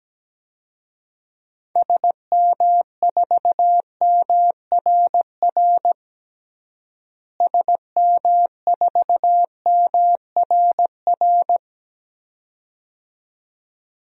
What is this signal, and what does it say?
Morse code, heard as a keyed tone, about 17 wpm, 700 Hz. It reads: SM4MRR SM4MRR
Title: DXCC-100 Call Signs - 6 Characters - Repeated 2x - 17wpm